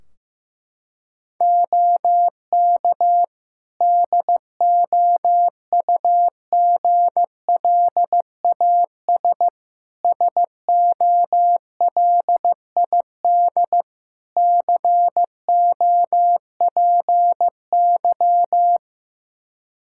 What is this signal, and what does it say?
Morse code, heard as a keyed tone, about 14 wpm, 700 Hz. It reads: OK DOUGLAS SOLID COPY
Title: Practice QSO - Exchange 3 - More Info - 15wpm